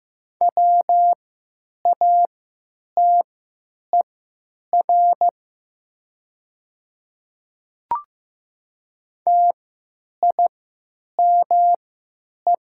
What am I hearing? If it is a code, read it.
Morse code, heard as a keyed tone, about 15 wpm, spaced as for 9 wpm, 700 Hz. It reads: WATER TIME
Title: 3x Character Spacing - Top 100 Words - 15wpm